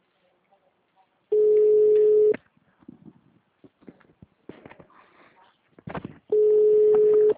Ringback tone (Europe, 425 Hz).